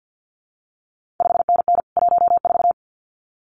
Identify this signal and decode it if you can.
Morse code, heard as a keyed tone, about 50 wpm, 700 Hz. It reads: 5NN 14
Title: CQ WW DX Contest Exchange - 50wpm